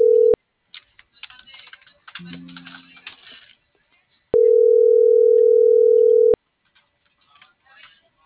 Ringback tone (North America), 440 + 480 Hz.